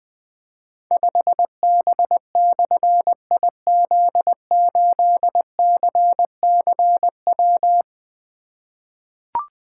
Morse code, heard as a keyed tone, about 20 wpm, 700 Hz. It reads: 5B/IZ8CCW
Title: Difficult Call Signs 20wpm